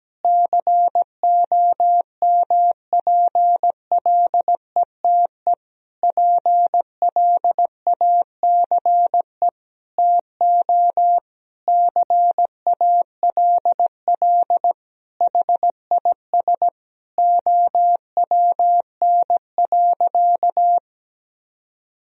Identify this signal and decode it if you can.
Morse code, heard as a keyed tone, about 17 wpm, 700 Hz. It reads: COMPLETE PLACE TO CALL HIS OWN.
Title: Sentences from Top 1000 Words - Review Part 2 - 17wpm